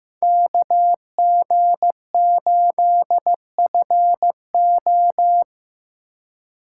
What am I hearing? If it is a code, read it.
Morse code, heard as a keyed tone, about 16 wpm, 700 Hz. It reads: KG8FO
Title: USA Call Signs - 4x Character Spacing - 15wpm